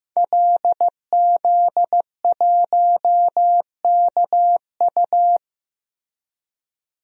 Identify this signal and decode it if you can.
Morse code, heard as a keyed tone, about 15 wpm, 700 Hz. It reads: LZ1KU
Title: DXCC-100 Call Signs - 5 Characters - Repeated 3x - 15wpm